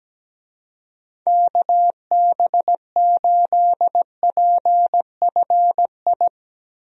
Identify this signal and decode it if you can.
Morse code, heard as a keyed tone, about 17 wpm, 700 Hz. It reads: KB8PFI
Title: USA Call Signs - 6 Characters - Repeated 2x - 17wpm